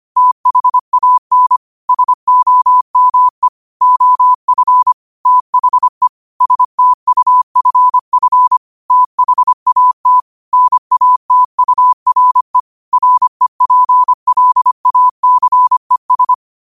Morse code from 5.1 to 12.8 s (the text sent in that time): THE STUFF THAT NATURE